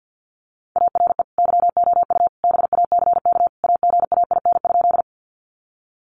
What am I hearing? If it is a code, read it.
Morse code, heard as a keyed tone, about 50 wpm, 700 Hz. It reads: ARE YOU BACK AGAIN?
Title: Sentences from Top 200 Words 50wpm